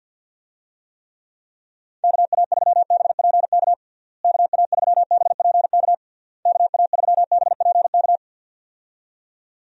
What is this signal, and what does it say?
Morse code, heard as a keyed tone, about 50 wpm, 700 Hz. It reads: KA3BPK KA3BPK KA3BPK
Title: USA Call Signs - 6 Characters - Repeated 3x - 50wpm